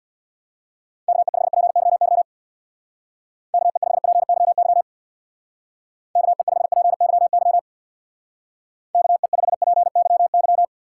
Morse code, heard as a keyed tone, about 60 wpm, 700 Hz. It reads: KE5PYY KE5PYY KE5PYY KE5PYY
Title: Speed Racing - USA 2x3 Call Signs 50wpm